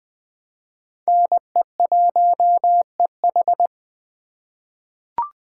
Morse code, heard as a keyed tone, about 20 wpm, 700 Hz. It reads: NE1EH
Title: USA Call Signs - 2x Character Spacing - 20wpm